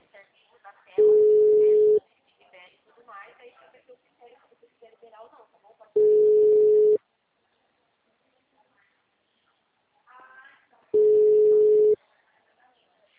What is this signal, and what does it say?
Ringback tone (Europe, 425 Hz), with speech.